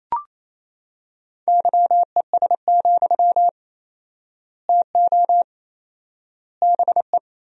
Morse code, heard as a keyed tone, about 28 wpm, 700 Hz. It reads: YES, TO BE
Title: 4x Word-Spacing - Sentences from Top 500 Words - 28wpm